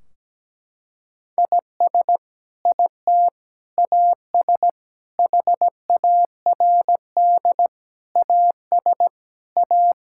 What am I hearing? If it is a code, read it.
Morse code, heard as a keyed tone, about 17 wpm, 700 Hz. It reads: IS IT AS HARD AS A
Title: Sentences from Top 1600 Words 17wpm